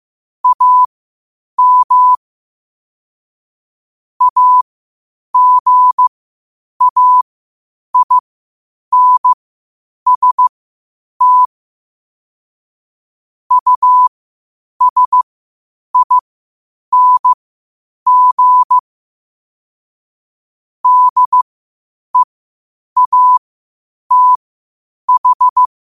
Morse code, heard as a keyed tone, about 15 wpm, spaced as for 8 wpm, 1000 Hz. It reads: AM AGAINST USING DEATH
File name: qotdfiles / 8wpm